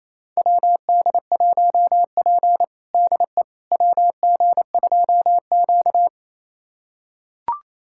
Morse code, heard as a keyed tone, about 28 wpm, 700 Hz. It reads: WD1P DE WG2Q
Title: USA Callsign DE Callsign - 4 Characters - 28wpm